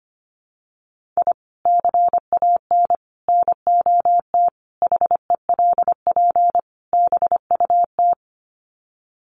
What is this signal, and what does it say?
Morse code, heard as a keyed tone, about 25 wpm, 700 Hz. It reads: I CAN NOT HELP BUT
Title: Sets of 5 Words Encore2 25wpm